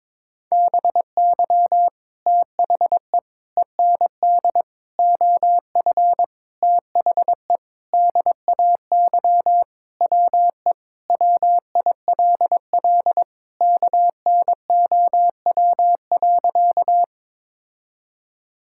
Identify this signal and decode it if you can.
Morse code, heard as a keyed tone, about 21 wpm, 700 Hz. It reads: BY THE END OF THE DAY WE WILL KNOW.
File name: practice / 22wpm / Sentences with 4-Word Phrases - 22wpm